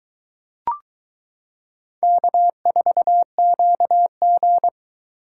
Morse code, heard as a keyed tone, about 22 wpm, 700 Hz. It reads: K4QG